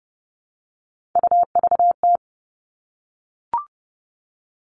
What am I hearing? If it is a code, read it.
Morse code, heard as a keyed tone, about 30 wpm, 700 Hz. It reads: UVT
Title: All Cut Numbers - 3-Digit 30wpm